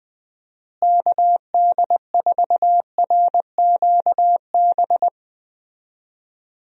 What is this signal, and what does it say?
Morse code, heard as a keyed tone, about 20 wpm, 700 Hz. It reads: KD4RQB
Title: USA Call Signs - 6 Characters - Repeated 3x - 20wpm